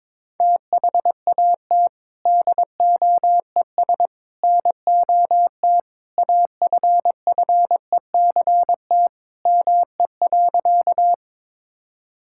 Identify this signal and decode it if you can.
Morse code, heard as a keyed tone, about 22 wpm, 700 Hz. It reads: THAT DOES NOT AFFECT ME.
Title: Sentences from Top 1000 Words - Review Part 6 - 22wpm